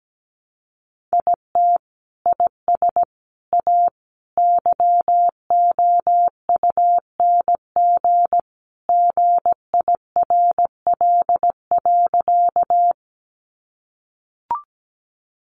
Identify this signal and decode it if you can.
Morse code, heard as a keyed tone, about 17 wpm, 700 Hz. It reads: IT IS A YOUNG GIRL.
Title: Sentences from Top 400 Words 17wpm